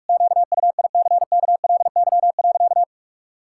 Morse code, heard as a keyed tone, about 44 wpm, 700 Hz. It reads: QUICKLY.
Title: Pangram Challenge v2 - 45wpm